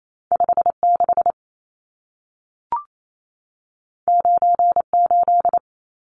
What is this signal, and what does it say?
Morse code, heard as a keyed tone, about 28 wpm, 700 Hz. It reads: 56 98
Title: Numbers - 2-Digit 28wpm